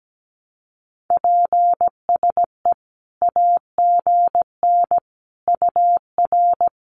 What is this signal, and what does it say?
Morse code, heard as a keyed tone, about 17 wpm, 700 Hz. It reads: PSE AGN UR
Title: Common QSO Phrases 17wpm